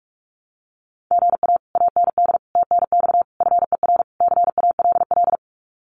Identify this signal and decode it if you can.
Morse code, heard as a keyed tone, about 44 wpm, 700 Hz. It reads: GA AND TNX FER CALL
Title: Common QSO Phrases Encore 45wpm